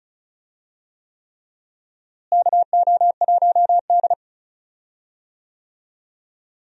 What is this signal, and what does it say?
Morse code, heard as a keyed tone, about 34 wpm, 700 Hz. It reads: KO1D
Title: USA Call Signs - 4 Characters - Repeated 3x - 35wpm